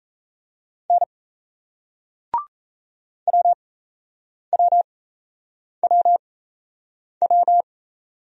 Morse code, heard as a keyed tone, about 38 wpm, spaced as for 8 wpm, 700 Hz. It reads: N WWWW